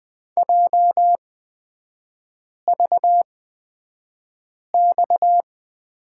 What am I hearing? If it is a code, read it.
Morse code, heard as a keyed tone, about 20 wpm, spaced as for 5 wpm, 690 Hz. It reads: JVX
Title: 229 Lesson 37e - Letter x - Three Characters 20wpm